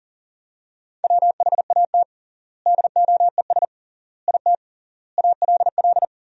Morse code, heard as a keyed tone, about 40 wpm, 700 Hz. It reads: WHAT DOES IT ALL
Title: 3x Word-Spacing - Sentences from Top 200 Words - 40wpm